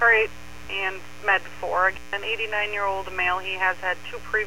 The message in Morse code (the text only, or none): none